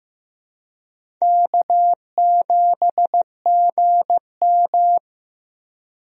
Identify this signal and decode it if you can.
Morse code, heard as a keyed tone, about 15 wpm, 700 Hz. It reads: K7GM
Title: USA Call Signs - 4 Characters - Repeated 2x - 15wpm